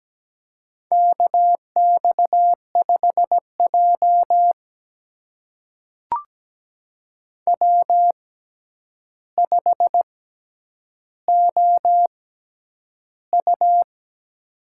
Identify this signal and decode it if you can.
Morse code, heard as a keyed tone, about 17 wpm, spaced as for 9 wpm, 700 Hz. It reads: KX5J W5OU
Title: USA Call Signs - 4 Characters - 6x Character Spacing - 17wpm